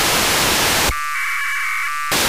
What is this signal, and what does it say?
Packet radio (AX.25, as in APRS).
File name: Horyu-4 10h28 UTC AFSK and CW JE9PEL